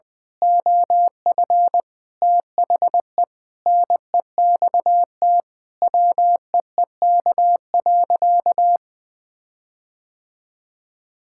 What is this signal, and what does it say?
Morse code, heard as a keyed tone, about 20 wpm, 700 Hz. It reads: OF THE NEXT WEEK.
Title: Sentences with 6-Word Phrases - 20wpm